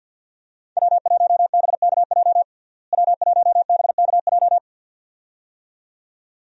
Morse code, heard as a keyed tone, about 53 wpm, 700 Hz. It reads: W1BKJ W1BKJ
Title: USA Call Signs - 5 Characters - Repeated 2x - 50wpm